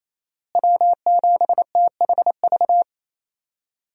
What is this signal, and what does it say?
Morse code, heard as a keyed tone, about 28 wpm, 700 Hz. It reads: W7THV
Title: USA Call Signs - 5 Characters - Repeated 2x - 28wpm